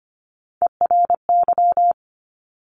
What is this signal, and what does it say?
Morse code, heard as a keyed tone, about 25 wpm, 700 Hz. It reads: ERY
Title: Top Three Letter Combinations in Words 25wpm